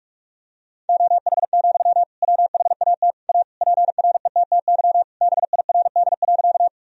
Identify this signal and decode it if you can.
Morse code, heard as a keyed tone, about 46 wpm, 700 Hz. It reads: OH, WHAT A PRETTY BIRD.
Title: Sentences from Top 1000 Words - Review Part 2 - 45wpm